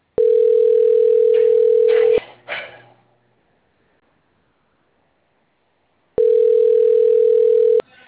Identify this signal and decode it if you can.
Ringback tone (North America), 440 + 480 Hz.
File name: out- - - - - .87359